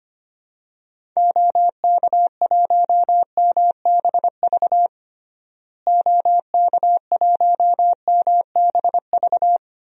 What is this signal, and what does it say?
Morse code, heard as a keyed tone, about 25 wpm, 700 Hz. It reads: OK1MBV OK1MBV
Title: DXCC-100 Call Signs - 6 Characters - Repeated 3x - 25wpm